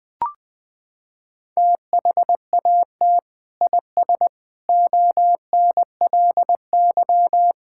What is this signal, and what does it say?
Morse code, heard as a keyed tone, about 21 wpm, 700 Hz. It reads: THAT IS ONLY
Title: Sentences from Top 1000 Words - Review Part 5 - 20wpm